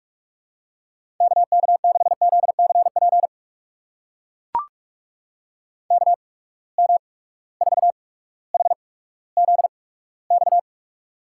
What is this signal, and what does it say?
Morse code, heard as a keyed tone, about 44 wpm, spaced as for 19 wpm, 700 Hz. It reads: KK6ZCP KM4HZX
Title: USA Call Signs - 6 Characters - 8x Character Spacing - 45wpm